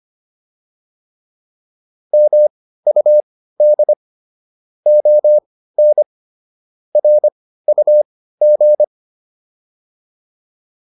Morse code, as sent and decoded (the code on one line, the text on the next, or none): -- ..- -.. / --- -. / .-. ..- --.
MUD ON RUG